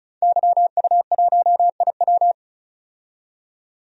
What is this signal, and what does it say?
Morse code, heard as a keyed tone, about 34 wpm, 700 Hz. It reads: YU1IW
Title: DXCC-100 Call Signs - 5 Characters - 35wpm